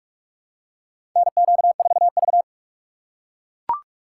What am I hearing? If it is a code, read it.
Morse code, heard as a keyed tone, about 41 wpm, 700 Hz. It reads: NQ4V
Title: USA Call Signs - 4 Characters - 8x Character Spacing - 45wpm